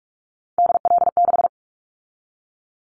Morse code, heard as a keyed tone, about 46 wpm, 700 Hz. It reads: DL6
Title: Callsign Prefixes for Top 50 DXCC - 45wpm